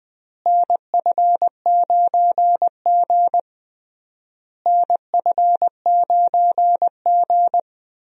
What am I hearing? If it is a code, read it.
Morse code, heard as a keyed tone, about 20 wpm, 700 Hz. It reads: NF9G NF9G
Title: USA Call Signs - 4 Characters - Repeated 3x - 20wpm